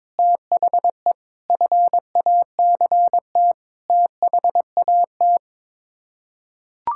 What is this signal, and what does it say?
Morse code, heard as a keyed tone, about 22 wpm, 700 Hz. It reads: THE FACT THAT
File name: practice / 22wpm / Sets of 8 Words - 22wpm